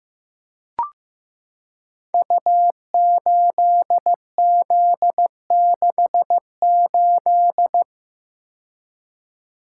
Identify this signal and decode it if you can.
Morse code, heard as a keyed tone, about 15 wpm, 700 Hz. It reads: U8Z68
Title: Code Groups - 5 Letter:Numbers - 15wpm